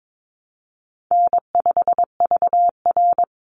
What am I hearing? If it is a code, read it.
Morse code, heard as a keyed tone, about 22 wpm, 700 Hz. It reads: N5VR
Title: Easy Call Signs - 22wpm